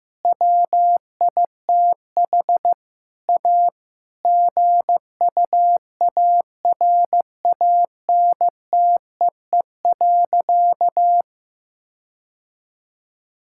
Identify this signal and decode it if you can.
Morse code, heard as a keyed tone, about 15 wpm, 700 Hz. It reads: WITH A GUARANTEE.
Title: Sentences from Top 1600 Words 15wpm